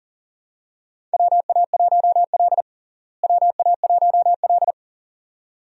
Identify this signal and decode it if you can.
Morse code, heard as a keyed tone, about 40 wpm, 700 Hz. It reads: WA1L WA1L
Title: USA Call Signs - 4 Characters - Repeated 2x - 40wpm